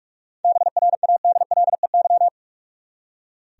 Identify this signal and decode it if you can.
Morse code, heard as a keyed tone, about 43 wpm, 700 Hz. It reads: BRADLEY
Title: USA Common Names - 45wpm